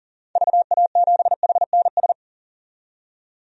Morse code, heard as a keyed tone, about 40 wpm, 700 Hz. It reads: VA7HNS
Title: DXCC-100 Call Signs - 6 Characters - 40wpm